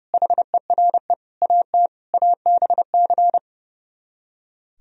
Morse code, heard as a keyed tone, about 28 wpm, 700 Hz. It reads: HERE AT ABC
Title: Speed-Racing - Sets of 5 Words 30wpm